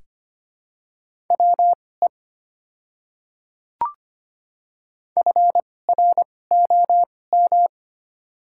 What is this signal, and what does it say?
Morse code, heard as a keyed tone, about 25 wpm, spaced as for 18 wpm, 690 Hz. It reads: WE FROM
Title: 2x Character Spacing - Top 100 Words - 25wpm